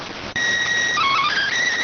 SSTV (slow-scan television), PD 290.